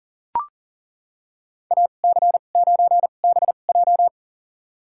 Morse code, heard as a keyed tone, about 37 wpm, spaced as for 28 wpm, 700 Hz. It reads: AC9BJ
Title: USA Call Signs - 5 Characters - 2x Character Spacing - 40wpm